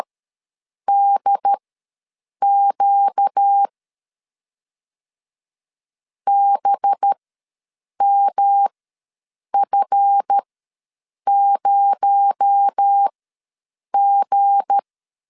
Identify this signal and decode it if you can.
Morse code, heard as a keyed tone, about 13 wpm, spaced as for 7 wpm, 800 Hz. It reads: DQ BMF0G
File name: LEZIONI / 7 GF9Z / 7 GF ws 12wpm 3140